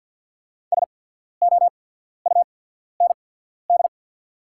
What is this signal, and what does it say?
Morse code, heard as a keyed tone, about 50 wpm, spaced as for 14 wpm, 690 Hz. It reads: SOUND
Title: 8x Character Spacing - Top 200 Words - 50wpm